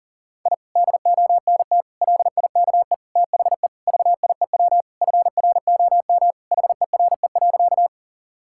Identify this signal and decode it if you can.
Morse code, heard as a keyed tone, about 40 wpm, 700 Hz. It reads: I DONT LIKE THE VIEW FROM HERE.
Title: Sentences from Top 1000 Words - Review Part 6 - 40wpm